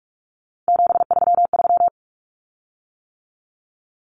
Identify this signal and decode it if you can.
Morse code, heard as a keyed tone, about 44 wpm, 700 Hz. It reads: 733
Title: Common Cut Numbers - 3-Digit 45wpm